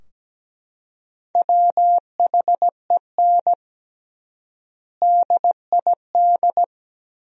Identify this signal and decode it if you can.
Morse code, heard as a keyed tone, about 17 wpm, 700 Hz. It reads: WHEN DID
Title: 3x Word-Spacing - Sentences from Top 400 Words - 17wpm